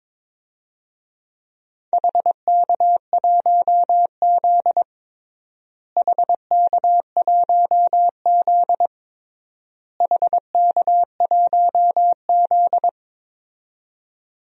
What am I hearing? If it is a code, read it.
Morse code, heard as a keyed tone, about 22 wpm, 700 Hz. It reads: HK1Z HK1Z HK1Z
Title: DXCC-100 Call Signs - 4 Characters - Repeated 3x - 22wpm